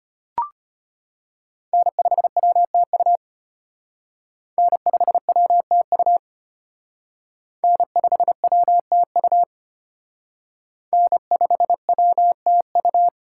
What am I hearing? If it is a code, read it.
Morse code, heard as a keyed tone, about 32 wpm, 700 Hz. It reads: N5WTU N5WTU N5WTU N5WTU